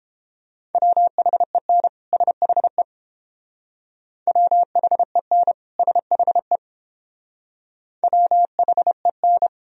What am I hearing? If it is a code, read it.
Morse code, heard as a keyed tone, about 29 wpm, 700 Hz. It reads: WHEN SHE WHEN SHE WHEN